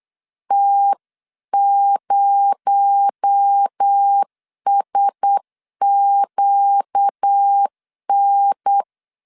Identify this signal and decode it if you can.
Morse code, heard as a keyed tone, about 8 wpm, 800 Hz. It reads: T0SQN
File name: 8wpm